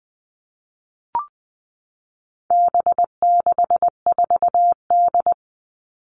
Morse code, heard as a keyed tone, about 20 wpm, 700 Hz. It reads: B64D
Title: All Cut Numbers - 4-Digit 20wpm